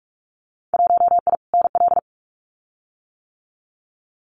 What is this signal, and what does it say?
Morse code, heard as a keyed tone, about 46 wpm, 690 Hz. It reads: 1I NL
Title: Winter Field Day Exchanges Encore 45wpm